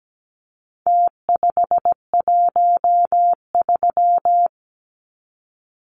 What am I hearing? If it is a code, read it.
Morse code, heard as a keyed tone, about 17 wpm, 700 Hz. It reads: T513
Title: Numbers - Contest Style - 4-Digit 17wpm